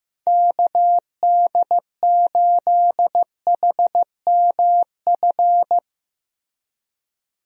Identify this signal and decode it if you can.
Morse code, heard as a keyed tone, about 15 wpm, 700 Hz. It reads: KD8HMF
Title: USA Call Signs - 6 Characters - Repeated 2x - 15wpm